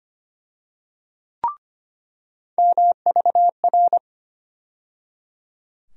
Morse code, heard as a keyed tone, about 25 wpm, 700 Hz. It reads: MVR
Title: USA Call Sign Postfixes - 3 Letters 25wpm